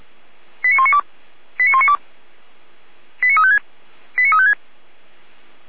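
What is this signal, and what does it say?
Five-tone selective call (ZVEI), heard as quick runs of five tones, tones of about 70 ms. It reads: 89192 89192 89356 89356
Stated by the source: ZVEI five-tone selective call 89013 received on SDR.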